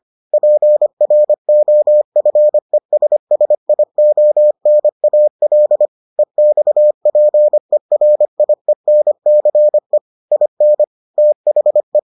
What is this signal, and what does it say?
Morse code, heard as a keyed tone, about 25 wpm, 600 Hz. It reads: PROFESSIONAL EXPERIENCE IN THE